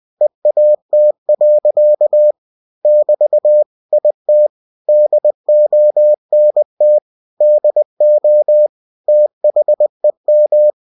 Morse code, heard as a keyed tone, about 20 wpm, 600 Hz. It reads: EAT. <BT> IT DONT DO THEM
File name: cw-books / winnie-the-pooh / winnie-20wpm-page-0084